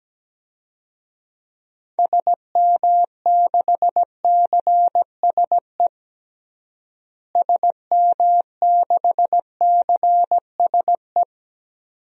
Morse code, heard as a keyed tone, about 17 wpm, 700 Hz. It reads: SM6CSE SM6CSE
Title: DXCC-100 Call Signs - 6 Characters - Repeated 2x - 17wpm